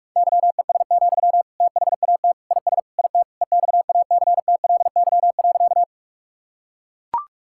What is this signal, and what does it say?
Morse code, heard as a keyed tone, about 41 wpm, 700 Hz. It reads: YES, THAT IS IT EXACTLY.
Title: Sentences from Top 1000 Words - Review Part 6 - 45wpm